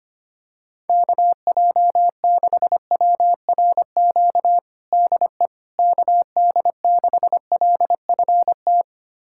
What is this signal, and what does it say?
Morse code, heard as a keyed tone, about 25 wpm, 700 Hz. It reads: KJ6WRQ DE KD6LFT
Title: USA Callsign DE Callsign - 6 Characters - 25wpm